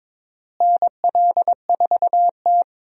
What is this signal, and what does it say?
Morse code, heard as a keyed tone, about 22 wpm, 700 Hz. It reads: NL4T
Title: USA Call Signs - 4 Characters - Repeated 3x - 22wpm